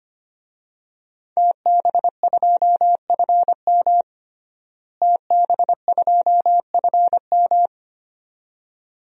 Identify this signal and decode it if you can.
Morse code, heard as a keyed tone, about 25 wpm, 700 Hz. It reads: TB2FM TB2FM
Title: DXCC-100 Call Signs - 5 Characters - Repeated 2x - 25wpm